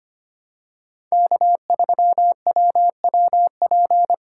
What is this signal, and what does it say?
Morse code, heard as a keyed tone, about 25 wpm, 700 Hz. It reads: K3WWP
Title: K1USN SST QSO Practice - 25wpm